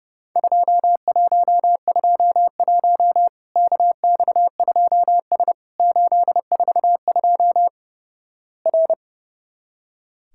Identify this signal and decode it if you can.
Morse code, heard as a keyed tone, about 30 wpm, 700 Hz. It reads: 2121 KX2S 842 R
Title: Worked All Europe DX Contest - QTC Exchange - 30wpm